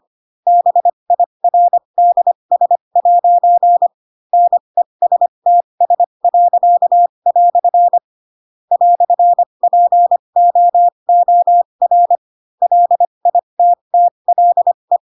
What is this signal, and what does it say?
Morse code, heard as a keyed tone, about 25 wpm, spaced as for 22 wpm, 700 Hz. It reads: BIRDS' NESTS." "POOR LITTLE
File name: Pinocchio / Pinocchio ch4 18 25 ik1hge